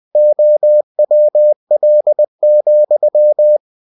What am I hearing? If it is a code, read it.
Morse code, heard as a keyed tone, about 20 wpm, 600 Hz. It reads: OWL,
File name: cw-books / winnie-the-pooh / winnie-20wpm-page-0102